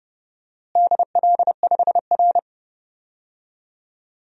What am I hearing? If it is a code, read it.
Morse code, heard as a keyed tone, about 30 wpm, 700 Hz. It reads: DL5R
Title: DXCC-100 Call Signs - 4 Characters - Repeated 3x - 30wpm